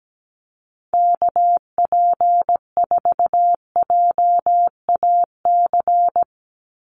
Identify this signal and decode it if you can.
Morse code, heard as a keyed tone, about 17 wpm, 700 Hz. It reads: KP4JAC
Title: USA Call Signs 17wpm